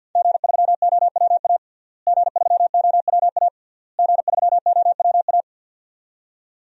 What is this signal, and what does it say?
Morse code, heard as a keyed tone, about 49 wpm, 700 Hz. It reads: G3OWA G3OWA G3OWA
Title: DXCC-100 Call Signs - 5 Characters - Repeated 3x - 50wpm